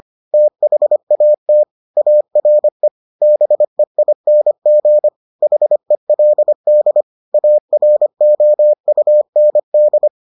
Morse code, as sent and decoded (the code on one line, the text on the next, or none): - .... .- - / .- .-. / -... .. -. --. / .... .-.. -.. / .- .-. --- ..- -. -..
THAT ARE BEING HELD AROUND